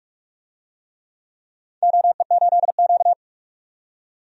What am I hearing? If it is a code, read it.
Morse code, heard as a keyed tone, about 47 wpm, 700 Hz. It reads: OE8Q
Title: DXCC-100 Call Signs - 4 Characters - 45wpm